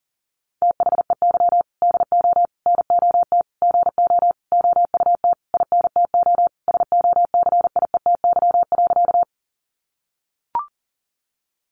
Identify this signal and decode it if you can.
Morse code, heard as a keyed tone, about 40 wpm, 700 Hz. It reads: THEY DO NOT GO OUT INTO SOCIETY.
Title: Sentences from Top 700 Words 40wpm